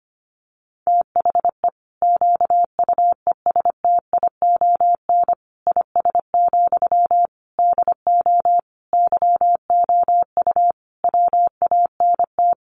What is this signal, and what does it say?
Morse code, heard as a keyed tone, about 25 wpm, 700 Hz. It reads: THE QUESTION IS, DO YOU WANT
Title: Sentences from Top 1000 Words 25wpm